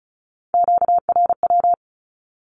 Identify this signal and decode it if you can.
Morse code, heard as a keyed tone, about 36 wpm, 700 Hz. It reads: QRW
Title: Q-Codes with Spoken Meaning - 35wpm